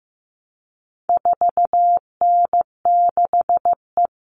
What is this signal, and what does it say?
Morse code, heard as a keyed tone, about 15 wpm, 700 Hz. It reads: 4N6E
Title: All Cut Numbers - 4-Digit 15wpm